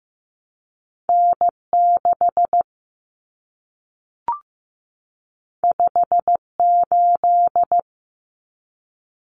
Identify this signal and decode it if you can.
Morse code, heard as a keyed tone, about 15 wpm, 700 Hz. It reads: N6 58
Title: Common Cut Numbers - 2-Digit 15wpm